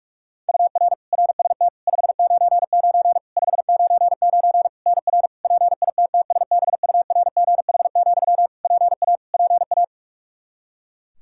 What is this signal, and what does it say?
Morse code, heard as a keyed tone, about 42 wpm, 700 Hz. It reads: UR RST 599 599 NR PITTSBURGH, PA PA
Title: Practice QSO - Exchange 1 - Beginning - 45wpm